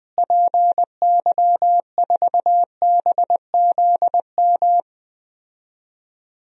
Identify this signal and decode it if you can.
Morse code, heard as a keyed tone, about 20 wpm, 700 Hz. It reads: PY4BZM
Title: DXCC-100 Call Signs - 6 Characters - Repeated 3x - 20wpm